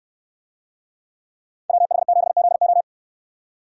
Morse code, heard as a keyed tone, about 69 wpm, 700 Hz. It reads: WH7ZO